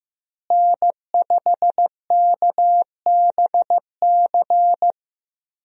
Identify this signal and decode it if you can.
Morse code, heard as a keyed tone, about 15 wpm, 700 Hz. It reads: N5KBC
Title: USA Call Signs - 5 Characters - Repeated 3x - 15wpm